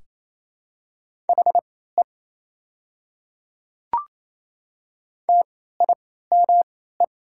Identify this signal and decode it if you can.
Morse code, heard as a keyed tone, about 28 wpm, spaced as for 16 wpm, 690 Hz. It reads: HE TIME
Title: 3x Character Spacing - Top 300 Words - 28wpm